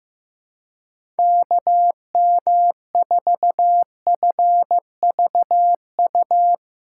Morse code, heard as a keyed tone, about 15 wpm, 700 Hz. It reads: KM4FVU
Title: USA Call Signs - 6 Characters - 6x Character Spacing - 15wpm